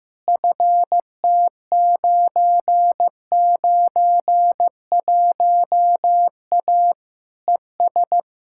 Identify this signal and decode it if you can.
Morse code, heard as a keyed tone, about 16 wpm, 700 Hz. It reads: FT991A ES
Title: Practice QSO - Exchange 2 - Info - 15wpm